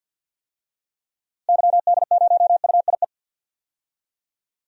Morse code, heard as a keyed tone, about 50 wpm, 700 Hz. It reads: YD0UIE
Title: DXCC-100 Call Signs - 6 Characters - Repeated 2x - 50wpm